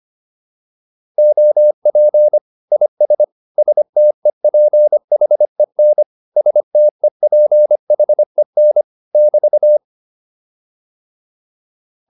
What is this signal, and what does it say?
Morse code, heard as a keyed tone, about 25 wpm, 600 Hz. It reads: OP IS STEPHEN STEPHEN <BT>